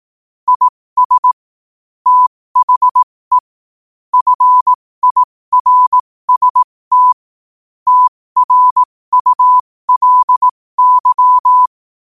Morse code, heard as a keyed tone, about 18 wpm, spaced as for 15 wpm, 1000 Hz. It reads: IS THE FIRST TRULY
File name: qotdfiles / 15wpm